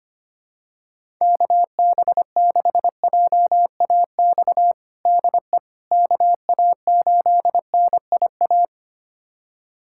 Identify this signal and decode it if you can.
Morse code, heard as a keyed tone, about 25 wpm, 700 Hz. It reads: KB6JAX DE KA8NIA
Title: USA Callsign DE Callsign - 6 Characters - 25wpm